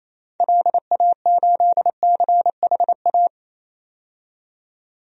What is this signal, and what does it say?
Morse code, heard as a keyed tone, about 28 wpm, 700 Hz. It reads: LA8CHA
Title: DXCC-100 Call Signs - 6 Characters - 28wpm